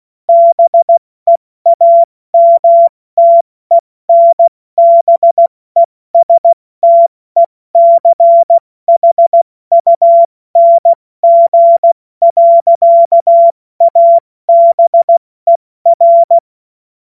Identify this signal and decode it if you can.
Morse code, heard as a keyed tone, about 16 wpm, 680 Hz. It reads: BEAMTENBESTECHUNG. ABER
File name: Morsekurs / Witze / Witz 01. März 2023 70 F80